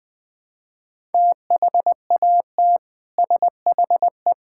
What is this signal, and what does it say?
Morse code, heard as a keyed tone, about 20 wpm, 700 Hz. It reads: THAT SHE
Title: Build A Phrase - 3-Word Phrases - 20wpm